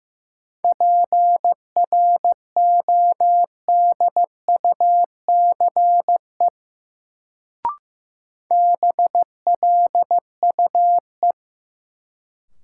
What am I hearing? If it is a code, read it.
Morse code, heard as a keyed tone, about 15 wpm, 700 Hz. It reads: PRODUCE BLUE
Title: Sentences from Top 1000 Words - Review Part 3 - New Word Drill - 15wpm